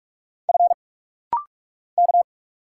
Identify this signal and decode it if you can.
Morse code, heard as a keyed tone, about 46 wpm, 700 Hz. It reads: F K
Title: Single Letter-Number Alt - Rapid-Fire 45wpm